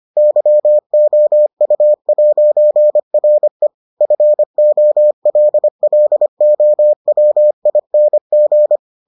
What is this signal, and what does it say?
Morse code, heard as a keyed tone, about 25 wpm, 600 Hz. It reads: YOU'RE FOLLOWING